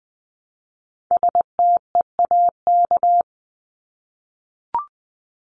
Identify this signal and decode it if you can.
Morse code, heard as a keyed tone, about 20 wpm, 700 Hz. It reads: STEAK
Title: Top 1251-1500 Words 20wpm